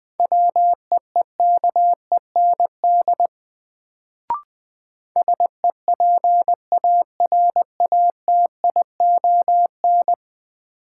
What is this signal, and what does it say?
Morse code, heard as a keyed tone, about 21 wpm, 700 Hz. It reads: WEEKEND SEPARATION
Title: Sets of 5 Words - New Word Drill - 20wpm